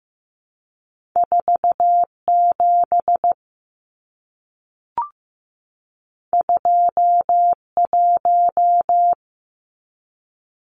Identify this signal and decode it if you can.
Morse code, heard as a keyed tone, about 15 wpm, 700 Hz. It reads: 47 21
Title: Numbers - 2-Digit 15wpm